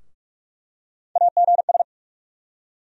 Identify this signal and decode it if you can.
Morse code, heard as a keyed tone, about 46 wpm, 700 Hz. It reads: AGS
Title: USA Call Sign Postfixes - 3 Letters 45wpm